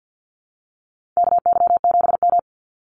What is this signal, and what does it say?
Morse code, heard as a keyed tone, about 50 wpm, 700 Hz. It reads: KY7M
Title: Call Signs Repeated 3x - 50wpm